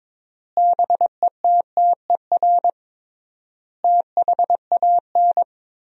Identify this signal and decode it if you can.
Morse code, heard as a keyed tone, about 22 wpm, 700 Hz. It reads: BETTER THAN
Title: 3x Word-Spacing - Sentences from Top 400 Words - 22wpm